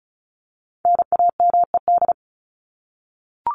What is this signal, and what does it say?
Morse code, heard as a keyed tone, about 34 wpm, 700 Hz. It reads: NAMED
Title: ED Suffix - Top 100 Words with Base Words 35wpm